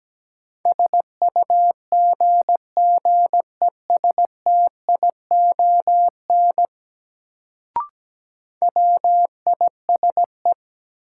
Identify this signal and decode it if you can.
Morse code, heard as a keyed tone, about 17 wpm, 700 Hz. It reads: SUGGESTION WISE
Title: Sentences from Top 1600 Words - New Word Drill - 17wpm